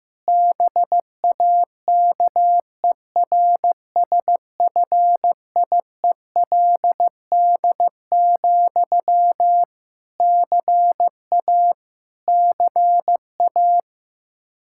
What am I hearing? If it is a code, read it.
Morse code, heard as a keyed tone, about 15 wpm, 700 Hz. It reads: BAKERSFIELD, CA CA
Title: Practice QSO - Exchange 1 - Beginning - 15wpm